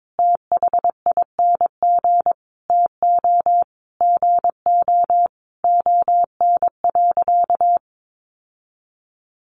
Morse code, heard as a keyed tone, about 22 wpm, 700 Hz. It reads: THING TO GO ON.
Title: Sentences from Top 1300 Words 22wpm